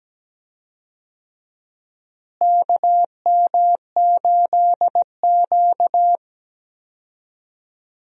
Morse code, heard as a keyed tone, about 17 wpm, 700 Hz. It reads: KM8Q
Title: USA Call Signs - 4 Characters - Repeated 2x - 17wpm